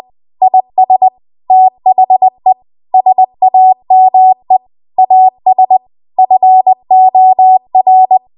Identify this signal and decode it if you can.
Morse code, heard as a keyed tone, about 20 wpm, 750 Hz. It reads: IS THE SAME AS FOR
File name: Archive / 20 WPM / 20WPM